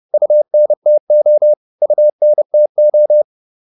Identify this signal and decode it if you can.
Morse code, heard as a keyed tone, about 30 wpm, 600 Hz. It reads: UNTO UNTO